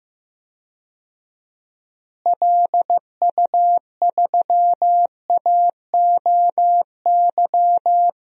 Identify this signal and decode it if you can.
Morse code, heard as a keyed tone, about 15 wpm, 700 Hz. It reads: LU3AOY
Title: DXCC-100 Call Signs - 6 Characters - Repeated 3x - 15wpm